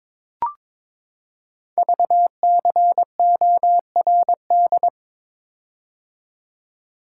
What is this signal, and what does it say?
Morse code, heard as a keyed tone, about 23 wpm, 700 Hz. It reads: VCORD
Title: Code Groups - 5 Letters - 22wpm